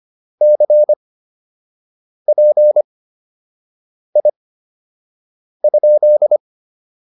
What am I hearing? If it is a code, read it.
Morse code, heard as a keyed tone, about 25 wpm, spaced as for 6 wpm, 600 Hz. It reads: CPI?